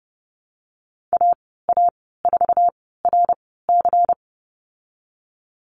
Morse code, heard as a keyed tone, about 30 wpm, spaced as for 17 wpm, 700 Hz. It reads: AA4RC
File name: Lesson 19g - Letter w - Call Signs 30wpm